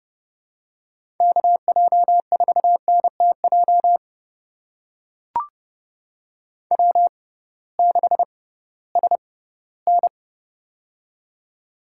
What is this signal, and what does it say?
Morse code, heard as a keyed tone, about 30 wpm, spaced as for 17 wpm, 700 Hz. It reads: KJ4NTJ W6SN
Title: USA Call Signs - 6x Character Spacing - 30wpm